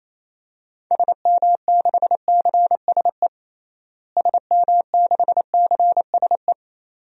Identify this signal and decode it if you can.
Morse code, heard as a keyed tone, about 28 wpm, 700 Hz. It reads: SM6CSE SM6CSE
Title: DXCC-100 Call Signs - 6 Characters - Repeated 2x - 28wpm